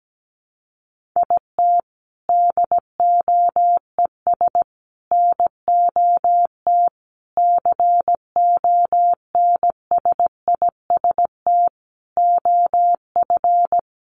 Morse code, heard as a keyed tone, about 17 wpm, 700 Hz. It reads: IT DOES NOT CONSIST OF
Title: Sentences from Top 1400 Words 17wpm